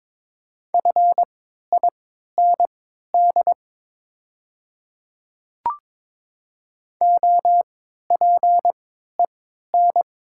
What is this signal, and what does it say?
Morse code, heard as a keyed tone, about 22 wpm, spaced as for 12 wpm, 700 Hz. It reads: FIND OPEN
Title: 3x Character Spacing - Top 300 Words - 22wpm